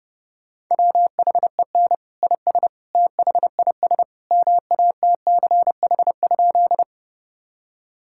Morse code, heard as a keyed tone, about 28 wpm, 700 Hz. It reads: WHEN IS THIS MATCH?
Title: Sentences from Top 1600 Words 30wpm